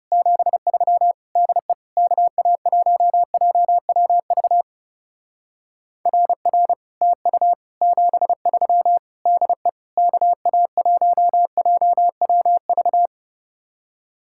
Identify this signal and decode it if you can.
Morse code, heard as a keyed tone, about 35 wpm, 700 Hz. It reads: 73 DE KA1JWV RR TU 73 DE KA1JWV
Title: Speed-Racing Common QSO Phrases Encore - 30wpm